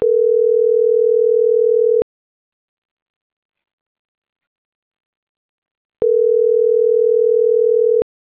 Ringback tone (North America), 440 + 480 Hz.
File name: CENTRAL MABE / out vanesa.mabe